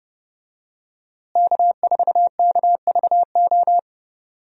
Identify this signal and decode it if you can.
Morse code, heard as a keyed tone, about 30 wpm, 700 Hz. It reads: K4KVO
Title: USA Call Signs - 5 Characters - Repeated 3x - 30wpm